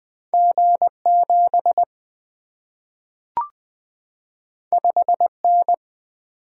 Morse code, heard as a keyed tone, about 20 wpm, 700 Hz. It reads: G 7 5 N
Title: Single-Letter Single-Number 20wpm